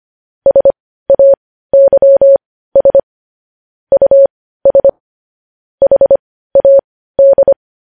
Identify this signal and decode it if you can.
Morse code, heard as a keyed tone, about 25 wpm, spaced as for 15 wpm, 550 Hz. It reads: SAYS US HAD
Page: news headlines in morse code at 15 wpm